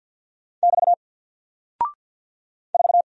Morse code, heard as a keyed tone, about 48 wpm, 700 Hz. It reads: <BT> 4
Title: Single Character (Letter:Number:Punctuation:Prosign) Alt 50wpm